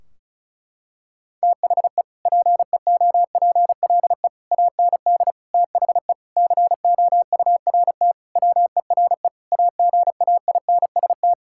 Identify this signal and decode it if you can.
Morse code, heard as a keyed tone, about 36 wpm, 700 Hz. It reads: THE PEOPLE AND THE COURT WERE AGAINST
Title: Sentences from Top 1600 Words 35wpm